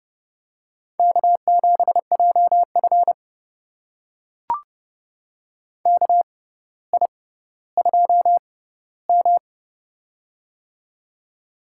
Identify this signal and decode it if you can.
Morse code, heard as a keyed tone, about 30 wpm, spaced as for 15 wpm, 700 Hz. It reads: K7JF KI2M
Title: USA Call Signs - 4 Characters - 6x Character Spacing - 30wpm